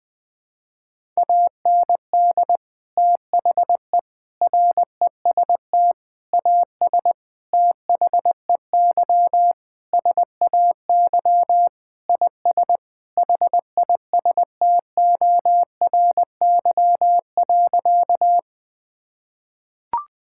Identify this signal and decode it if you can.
Morse code, heard as a keyed tone, about 21 wpm, 700 Hz. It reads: AND THE REST AS THEY SAY IS HISTORY.
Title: Sentences with 8-Word Phrases - 20wpm